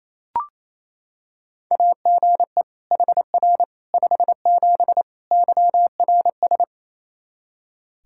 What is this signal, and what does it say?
Morse code, heard as a keyed tone, about 28 wpm, 700 Hz. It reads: AGE HR 57 YRS
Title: Practice QSO - Exchange 3 - More Info - 28wpm